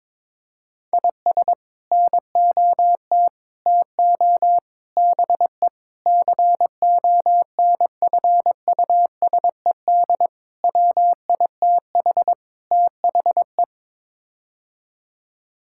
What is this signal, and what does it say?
Morse code, heard as a keyed tone, about 22 wpm, 700 Hz. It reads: IS NOT TO BE CONFUSED WITH THE
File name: practice / 22wpm / Sets of 7 Words - 22wpm